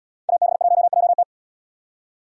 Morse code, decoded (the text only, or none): AB1JT